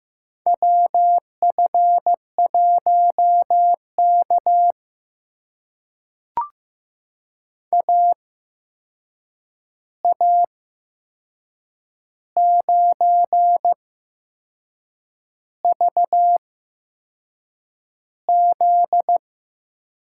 Morse code, heard as a keyed tone, about 15 wpm, spaced as for 6 wpm, 700 Hz. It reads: WF1K AA9VZ
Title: USA Call Signs - 8x Character Spacing - 15wpm